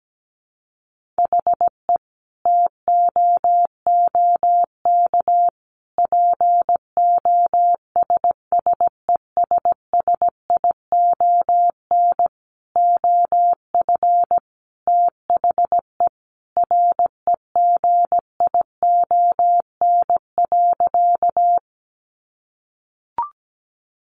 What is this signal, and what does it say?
Morse code, heard as a keyed tone, about 17 wpm, 700 Hz. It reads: HE TOOK POSSESSION OF THE REGION.
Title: Sentences from Top 1500 Words 17wpm